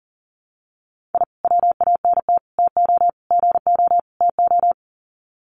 Morse code, heard as a keyed tone, about 40 wpm, 700 Hz. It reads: I WANT TO GO TO
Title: Sets of 5 Words Encore 40wpm